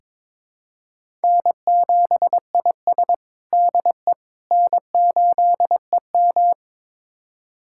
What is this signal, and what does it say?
Morse code, heard as a keyed tone, about 22 wpm, 700 Hz. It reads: N7IS DE N8EM
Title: USA Callsign DE Callsign - 4 Characters - 22wpm